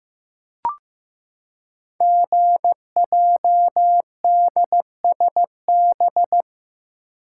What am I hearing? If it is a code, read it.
Morse code, heard as a keyed tone, about 15 wpm, 690 Hz. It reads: GJDSB
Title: Code Groups - 5 Letters - 15wpm